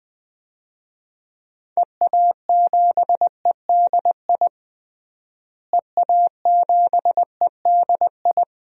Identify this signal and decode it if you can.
Morse code, heard as a keyed tone, about 20 wpm, 700 Hz. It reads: EA7EDI EA7EDI
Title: DXCC-100 Call Signs - 6 Characters - Repeated 2x - 20wpm